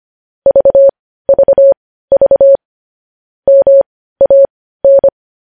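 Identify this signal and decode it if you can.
Morse code, heard as a keyed tone, about 25 wpm, spaced as for 15 wpm, 550 Hz. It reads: VVV MAN
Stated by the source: Morse practice sending news headlines at 15 wpm.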